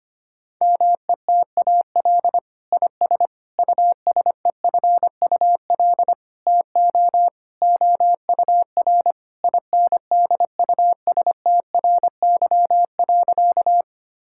Morse code, heard as a keyed tone, about 25 wpm, 700 Hz. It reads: METAL IS USEFUL TO OUR INDUSTRY.
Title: Sentences from Top 1000 Words - Review Part 6 - 25wpm